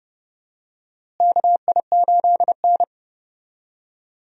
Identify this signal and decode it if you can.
Morse code, heard as a keyed tone, about 32 wpm, 700 Hz. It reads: KI8N
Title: USA Call Signs - 4 Characters - Repeated 2x - 30wpm